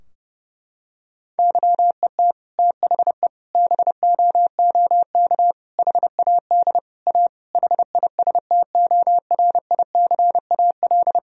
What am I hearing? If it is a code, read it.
Morse code, heard as a keyed tone, about 30 wpm, 700 Hz. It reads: YET THE BOOK HAD A HISTORICAL
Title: Sentences from Top 1000 Words - Review Part 4 - 30wpm